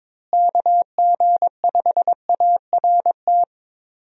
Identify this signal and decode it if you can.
Morse code, heard as a keyed tone, about 22 wpm, 700 Hz. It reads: KG5ART
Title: USA Call Signs - 6 Characters - Repeated 2x - 22wpm